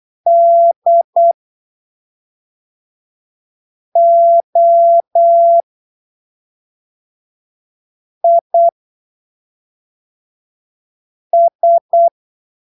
Morse code, heard as a keyed tone, about 8 wpm, spaced as for 3 wpm, 680 Hz. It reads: DOIS
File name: arquivos / texto / texto 04 08ppm